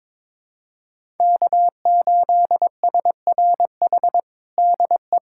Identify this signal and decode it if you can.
Morse code, heard as a keyed tone, about 22 wpm, 700 Hz. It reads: K8SRH DE
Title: USA Callsign DE Callsign 22wpm